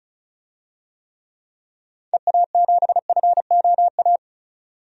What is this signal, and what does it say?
Morse code, heard as a keyed tone, about 36 wpm, 700 Hz. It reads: EA7FOA
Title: DXCC-100 Call Signs - 6 Characters - 35wpm